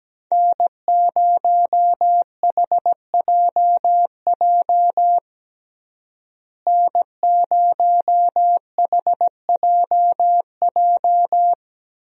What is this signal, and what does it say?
Morse code, heard as a keyed tone, about 17 wpm, 700 Hz. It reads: N0HJJ N0HJJ
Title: USA Call Signs - 5 Characters - Repeated 2x - 17wpm